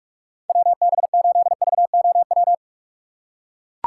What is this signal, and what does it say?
Morse code, heard as a keyed tone, about 41 wpm, 690 Hz. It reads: WB8VOW
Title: USA Call Signs - 6 Characters - 3x Character Spacing - 45wpm